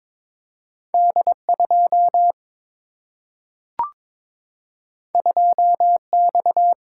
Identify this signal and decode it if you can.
Morse code, heard as a keyed tone, about 22 wpm, 700 Hz. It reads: D2 2X
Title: Single-Letter Single-Number 22wpm